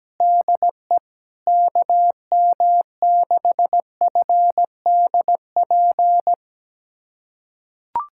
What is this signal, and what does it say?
Morse code, heard as a keyed tone, about 17 wpm, 700 Hz. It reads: DE KM6FDP
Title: USA Callsign DE Callsign - 6 Characters - 17wpm